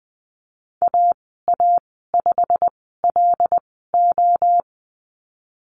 Morse code, heard as a keyed tone, about 20 wpm, spaced as for 15 wpm, 700 Hz. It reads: AA5LO
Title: 088 Lesson 15g - Number 5 - Call Signs 20wpm